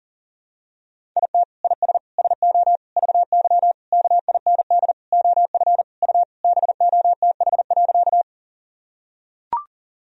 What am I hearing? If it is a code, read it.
Morse code, heard as a keyed tone, about 40 wpm, 700 Hz. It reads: IT IS SO VY KIND OF U BOTH.
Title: Sentences from Top 300 Words and Abbreviations 40wpm